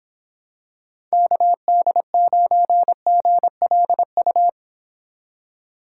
Morse code, heard as a keyed tone, about 26 wpm, 700 Hz. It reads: KD9GLU